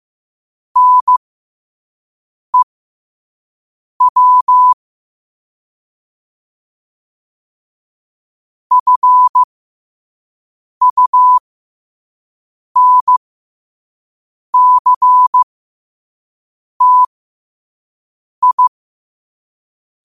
Morse code, heard as a keyed tone, about 15 wpm, spaced as for 5 wpm, 1000 Hz. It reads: NEW FUNCTI
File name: qotdfiles / 5wpm